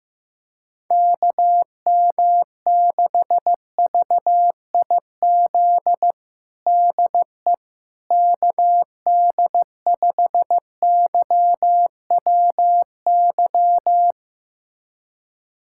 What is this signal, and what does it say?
Morse code, heard as a keyed tone, about 16 wpm, 700 Hz. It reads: KM6VIZ DE KD5YWY
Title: USA Callsign DE Callsign - 6 Characters - 15wpm